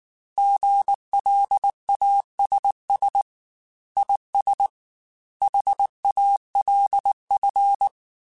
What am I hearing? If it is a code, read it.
Morse code, heard as a keyed tone, about 19 wpm, 800 Hz. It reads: GLASS IS HALF
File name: Chuckles / Chuckles-mp3s / Glass Half Full @18